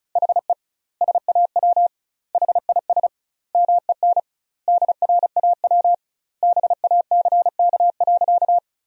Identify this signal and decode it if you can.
Morse code, heard as a keyed tone, about 34 wpm, 700 Hz. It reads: HE SAW HIS MEN DRAW BACK.
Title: 2x Word-Spacing - Sentences from Top 300 Words - 35wpm